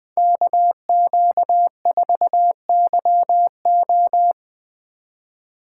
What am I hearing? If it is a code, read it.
Morse code, heard as a keyed tone, about 21 wpm, 700 Hz. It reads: KQ4YO
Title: Speed Racing - USA 2x2 Call Signs 20wpm